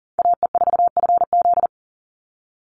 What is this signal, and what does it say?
Morse code, heard as a keyed tone, about 40 wpm, 700 Hz. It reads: AE4FZ
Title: NA SOTA Chaser Callsigns 40wpm